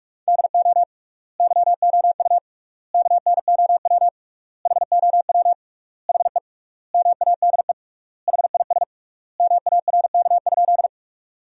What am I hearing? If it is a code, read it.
Morse code, heard as a keyed tone, about 44 wpm, 700 Hz. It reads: DO YOU KNOW HOW HE MADE HIS MARK?
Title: 3x Word-Spacing - Sentences from Top 300 Words - 45wpm